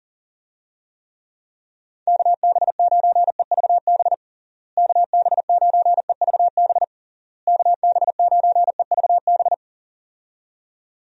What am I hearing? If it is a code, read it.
Morse code, heard as a keyed tone, about 37 wpm, 700 Hz. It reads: KB9EVB KB9EVB KB9EVB
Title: USA Call Signs - 6 Characters - Repeated 3x - 40wpm